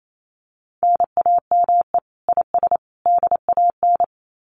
Morse code, heard as a keyed tone, about 28 wpm, 700 Hz. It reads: NAME IS DAN
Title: Common QSO Phrases 28wpm